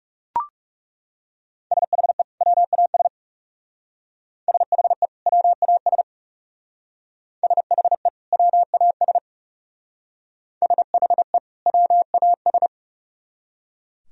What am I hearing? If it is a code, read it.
Morse code, heard as a keyed tone, about 36 wpm, 700 Hz. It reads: SHE WAS SHE WAS SHE WAS SHE WAS
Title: Speed-Racing - Sets of 2 Words 30wpm